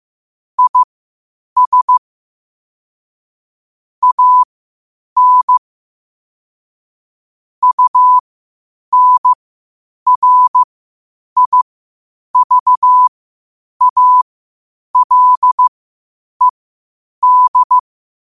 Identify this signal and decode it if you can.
Morse code, heard as a keyed tone, about 15 wpm, spaced as for 8 wpm, 1000 Hz. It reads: IS AN UNRIVALED
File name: qotdfiles / 8wpm